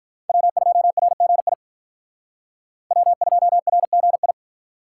Morse code, heard as a keyed tone, about 46 wpm, 700 Hz. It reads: W2RGI W2RGI